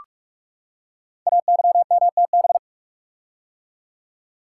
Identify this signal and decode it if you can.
Morse code, heard as a keyed tone, about 44 wpm, 700 Hz. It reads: AYMTB
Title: Code Groups - 5 Letters - 45wpm